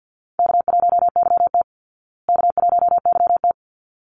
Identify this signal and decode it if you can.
Morse code, heard as a keyed tone, about 50 wpm, 700 Hz. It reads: K1YT K1YT
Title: Easy Call Signs Repeated 2x - 50wpm